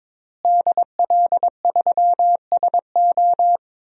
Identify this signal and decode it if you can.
Morse code, heard as a keyed tone, about 21 wpm, 700 Hz. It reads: DL3SO
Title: DXCC-100 Call Signs - 5 Characters - Repeated 3x - 22wpm